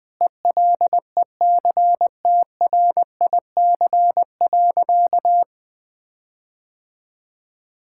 Morse code, heard as a keyed tone, about 20 wpm, 700 Hz. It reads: ELECTRIC.
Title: Difficult Words - Speed-Racing with Hints - 20wpm